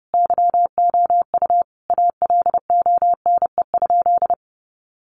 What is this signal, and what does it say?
Morse code, heard as a keyed tone, about 30 wpm, 700 Hz. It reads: YOU ALONE?
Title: Sentences from Top 1000 Words 30wpm